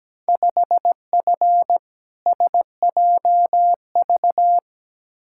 Morse code, heard as a keyed tone, about 17 wpm, 700 Hz. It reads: 5F SJV
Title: Field Day Exchanges Encore - 17wpm